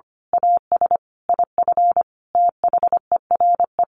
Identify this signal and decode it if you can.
Morse code, heard as a keyed tone, about 25 wpm, 700 Hz. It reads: AS IF THERE
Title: A Sound Of Thunder 25wpm